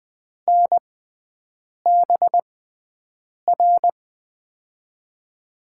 Morse code, heard as a keyed tone, about 21 wpm, spaced as for 7 wpm, 700 Hz. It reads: NBR
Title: USA Call Signs - 6x Character Spacing - 20wpm